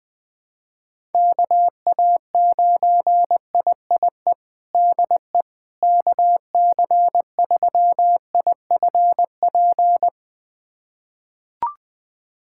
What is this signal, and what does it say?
Morse code, heard as a keyed tone, about 20 wpm, 700 Hz. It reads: KA9IIE DE KC3IFP
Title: USA Callsign DE Callsign 20wpm